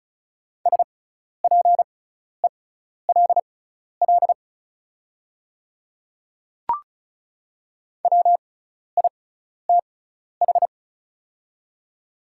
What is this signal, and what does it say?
Morse code, heard as a keyed tone, about 36 wpm, spaced as for 12 wpm, 690 Hz. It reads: SPELL WITH
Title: 6x Character Spacing - Top 200 Words - 35wpm